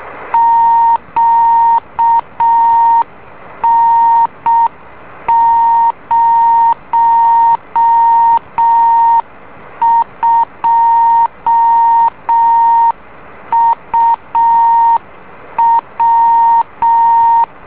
Morse code, heard as a keyed tone, about 6 wpm, 920 Hz. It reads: QN02UW
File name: beacon / JH8ZND